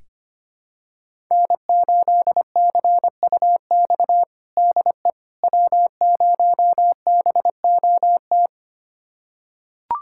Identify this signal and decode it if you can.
Morse code, heard as a keyed tone, about 25 wpm, 700 Hz. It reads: N8CUX DE W0BOT
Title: USA Callsign DE Callsign - 5 Characters - 25wpm